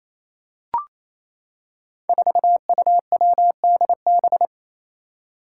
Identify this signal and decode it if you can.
Morse code, heard as a keyed tone, about 28 wpm, 700 Hz. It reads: 4UWDB
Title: Five Challenging Letter-Number Combinations 28wpm